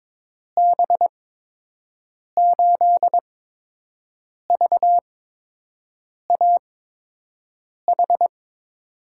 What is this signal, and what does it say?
Morse code, heard as a keyed tone, about 22 wpm, spaced as for 6 wpm, 700 Hz. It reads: B8VAH
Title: USA Call Signs - 6 Characters - 8x Character Spacing - 22wpm